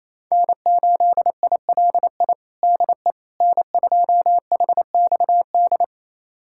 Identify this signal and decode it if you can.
Morse code, heard as a keyed tone, about 28 wpm, 700 Hz. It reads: N8ILI DE N2HXD
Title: USA Callsign DE Callsign - 5 Characters - 28wpm